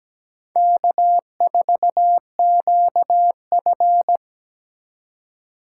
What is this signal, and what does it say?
Morse code, heard as a keyed tone, about 17 wpm, 700 Hz. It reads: K4QF
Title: USA Call Signs - 4x Character Spacing - 17wpm